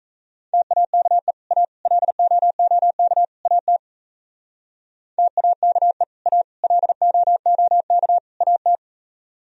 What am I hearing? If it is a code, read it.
Morse code, heard as a keyed tone, about 35 wpm, 700 Hz. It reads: TAKE A LOOK AT TAKE A LOOK AT